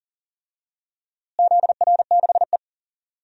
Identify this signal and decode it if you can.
Morse code, heard as a keyed tone, about 40 wpm, 700 Hz. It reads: ZR6E
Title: DXCC-100 Call Signs - 4 Characters - Repeated 3x - 40wpm